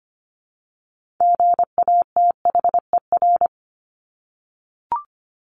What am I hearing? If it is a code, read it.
Morse code, heard as a keyed tone, about 25 wpm, 700 Hz. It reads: GATHER
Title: Top 1501-1750 Words 25wpm